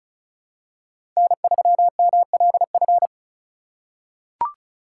Morse code, heard as a keyed tone, about 36 wpm, 690 Hz. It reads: N3MLF
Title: USA Call Signs - 5 Characters - 3x Character Spacing - 35wpm